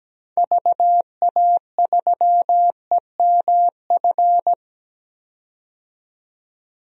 Morse code, heard as a keyed tone, about 17 wpm, 700 Hz. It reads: VA3EMF
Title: DXCC-100 Call Signs - 6 Characters - 17wpm